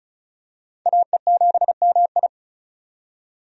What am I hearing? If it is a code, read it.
Morse code, heard as a keyed tone, about 36 wpm, 700 Hz. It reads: AE7MI
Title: USA Call Signs - 5 Characters - 35wpm